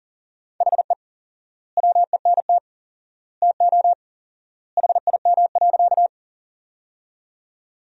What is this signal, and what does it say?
Morse code, heard as a keyed tone, about 40 wpm, 700 Hz. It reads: HE WENT TO HIM.
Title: 4x Word-Spacing - Sentences from Top 200 Words - 40wpm